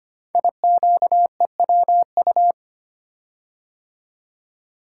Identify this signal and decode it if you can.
Morse code, heard as a keyed tone, about 25 wpm, 700 Hz. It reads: IQEWU
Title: Code Groups - 5 Letters - 25wpm